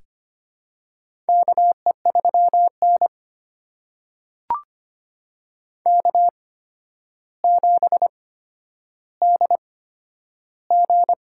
Morse code, heard as a keyed tone, about 25 wpm, spaced as for 11 wpm, 700 Hz. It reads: KE3N K7DG
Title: USA Call Signs - 4 Characters - 8x Character Spacing - 25wpm